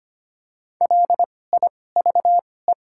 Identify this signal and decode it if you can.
Morse code, heard as a keyed tone, about 25 wpm, spaced as for 18 wpm, 700 Hz. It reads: LIVE
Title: 2x Character Spacing - Top 300 Words - 25wpm